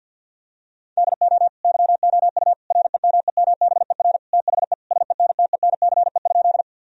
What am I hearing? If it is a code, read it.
Morse code, heard as a keyed tone, about 52 wpm, 700 Hz. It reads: DO YOU REMEMBER THE SENTENCE?
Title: Sentences from Top 1000 Words - Review Part 1 - 50wpm